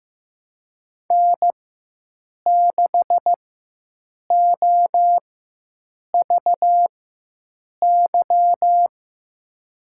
Morse code, heard as a keyed tone, about 15 wpm, spaced as for 7 wpm, 700 Hz. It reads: N6OVY
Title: USA Call Signs - 5 Characters - 4x Character Spacing - 15wpm